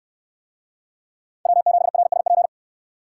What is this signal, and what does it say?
Morse code, heard as a keyed tone, about 67 wpm, 700 Hz. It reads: W7RSP